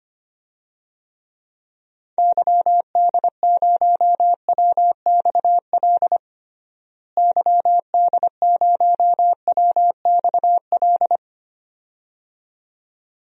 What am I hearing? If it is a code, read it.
Morse code, heard as a keyed tone, about 25 wpm, 700 Hz. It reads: YD0WXL YD0WXL
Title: DXCC-100 Call Signs - 6 Characters - Repeated 2x - 25wpm